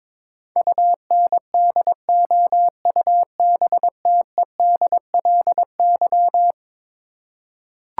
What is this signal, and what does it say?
Morse code, heard as a keyed tone, about 21 wpm, 700 Hz. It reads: UNDOUBTEDLY
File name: practice / 22wpm / Eleven Letter Words - 22wpm